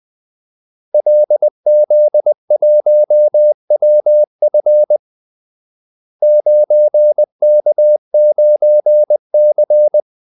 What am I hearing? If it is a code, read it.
Morse code, heard as a keyed tone, about 20 wpm, 600 Hz. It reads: LZ1WF 9K9C
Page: Kaldesignaler 20 wpm